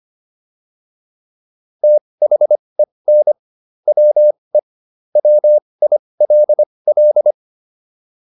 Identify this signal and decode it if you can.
Morse code, heard as a keyed tone, about 25 wpm, spaced as for 20 wpm, 600 Hz. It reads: THEN WE WILL